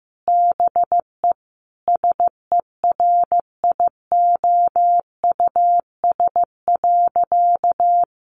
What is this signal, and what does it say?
Morse code, heard as a keyed tone, about 15 wpm, 700 Hz. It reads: BE SERIOUS.
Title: Sentences from Top 1100 Words 15wpm